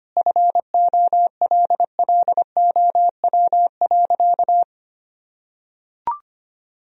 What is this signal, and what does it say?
Morse code, heard as a keyed tone, about 25 wpm, 700 Hz. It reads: FOLLOW.
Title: Sentences from Top 1000 Words - Review Part 1 - 25wpm